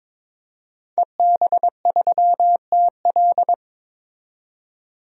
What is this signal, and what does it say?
Morse code, heard as a keyed tone, about 22 wpm, 700 Hz. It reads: EB3TL
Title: DXCC-100 Call Signs - 5 Characters - 22wpm